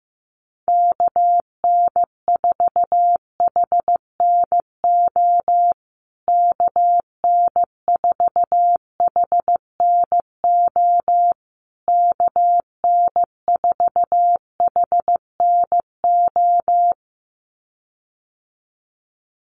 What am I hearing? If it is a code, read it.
Morse code, heard as a keyed tone, about 15 wpm, 700 Hz. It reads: KN4HNO KN4HNO KN4HNO
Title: USA Call Signs Repeated 3x 15wpm